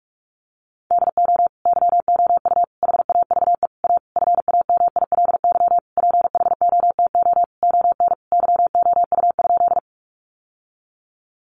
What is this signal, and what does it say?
Morse code, heard as a keyed tone, about 44 wpm, 700 Hz. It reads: DO YOU HAVE A FAMILY PHOTO ON YOU?
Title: Sentences from Top 1200 Words 45wpm